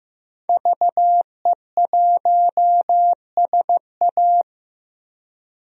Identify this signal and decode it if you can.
Morse code, heard as a keyed tone, about 15 wpm, 700 Hz. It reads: VE1SA
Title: DXCC-100 Call Signs - 5 Characters - 15wpm